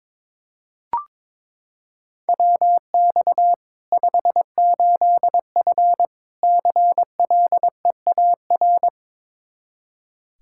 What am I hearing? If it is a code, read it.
Morse code, heard as a keyed tone, about 22 wpm, 700 Hz. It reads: WX 58F CLEAR
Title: Practice QSO - Exchange 2 - Info - 22wpm